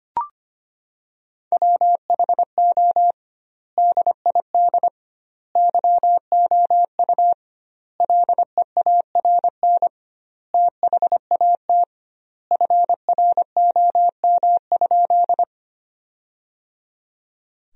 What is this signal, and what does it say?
Morse code, heard as a keyed tone, about 25 wpm, 700 Hz. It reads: WHO DID YOU LEARN THAT FROM?
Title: 2x Word-Spacing - Sentences from Top 300 Words - 25wpm